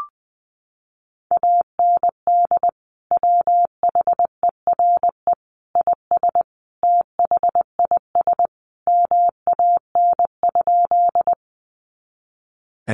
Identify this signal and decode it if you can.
Morse code, heard as a keyed tone, about 20 wpm, 700 Hz. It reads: AND WHERE IS THIS MAN?
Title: Sentences from Top 200 Words 20wpm